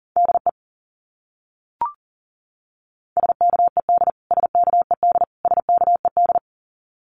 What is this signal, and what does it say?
Morse code, heard as a keyed tone, about 37 wpm, 700 Hz. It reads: DE SKED SKED SKED
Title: Abbreviations Repeated 3x - 40wpm